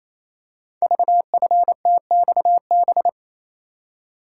Morse code, heard as a keyed tone, about 28 wpm, 690 Hz. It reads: VFTXB
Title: Code Groups - 5 Letters - 28wpm